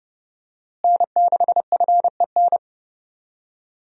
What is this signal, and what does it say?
Morse code, heard as a keyed tone, about 30 wpm, 700 Hz. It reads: N6FEN
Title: USA Call Signs - 5 Characters - Repeated 3x - 30wpm